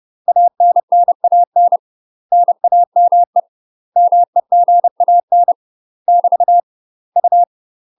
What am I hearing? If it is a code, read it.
Morse code, heard as a keyed tone, about 30 wpm, 700 Hz. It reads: ANNAN NAME MEGAN <BT> U
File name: UK MASTER QSOs TRAINING / 30WPM / UK MASTER QSOs 0043